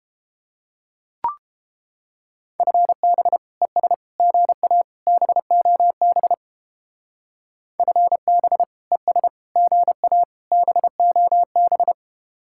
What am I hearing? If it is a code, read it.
Morse code, heard as a keyed tone, about 31 wpm, 700 Hz. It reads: FB ES GA BOB FB ES GA BOB